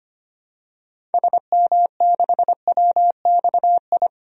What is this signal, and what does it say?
Morse code, heard as a keyed tone, about 25 wpm, 700 Hz. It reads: SM6WXI
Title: DXCC-100 Call Signs - 6 Characters - Repeated 2x - 25wpm